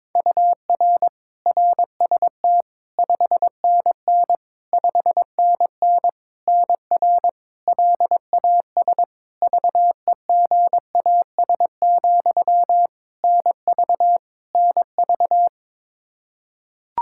Morse code, heard as a keyed tone, about 22 wpm, 700 Hz. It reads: UR RST 5NN 5NN NR LAS VEGAS, NV NV
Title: K1USN SST QSO Practice 22wpm